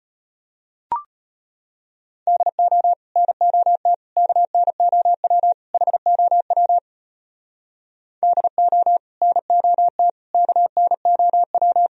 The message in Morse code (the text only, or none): DO NOT KNOW HOW DO NOT KNOW